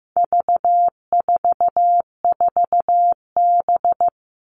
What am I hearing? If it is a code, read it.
Morse code, heard as a keyed tone, about 15 wpm, 700 Hz. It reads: V44B
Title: All Cut Numbers - 4-Digit 15wpm